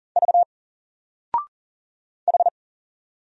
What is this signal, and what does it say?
Morse code, heard as a keyed tone, about 40 wpm, 700 Hz. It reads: V H
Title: Single Letter-Number 40wpm